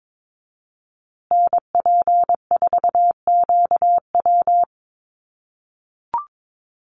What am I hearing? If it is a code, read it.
Morse code, heard as a keyed tone, about 22 wpm, 700 Hz. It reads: NP4QW
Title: USA Call Signs Repeated 3x 22wpm